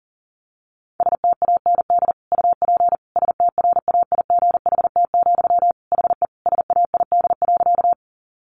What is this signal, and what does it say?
Morse code, heard as a keyed tone, about 40 wpm, 700 Hz. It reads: STAND UP STRAIGHT, HE SAID.
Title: Sentences from Top 1000 Words 40wpm